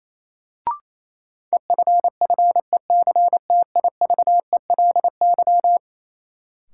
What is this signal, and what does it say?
Morse code, heard as a keyed tone, about 28 wpm, 700 Hz. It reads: EFFECTIVELY
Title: Sentences from Top 1400 Words - New Word Drill - 28wpm